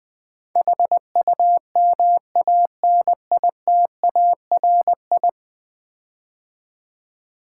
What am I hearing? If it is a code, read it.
Morse code, heard as a keyed tone, about 20 wpm, 700 Hz. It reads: HUMANITARI
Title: Build A Word - 5 Letters At A Time - 12-Letter Words - 20wpm